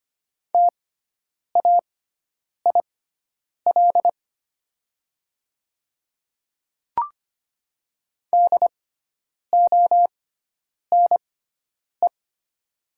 Morse code, heard as a keyed tone, about 25 wpm, spaced as for 9 wpm, 700 Hz. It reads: TAIL DONE
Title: 6x Character Spacing - Top 500 Words - 25wpm